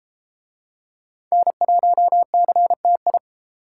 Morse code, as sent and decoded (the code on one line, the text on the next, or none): -. .---- -.-. - ..
N1CTI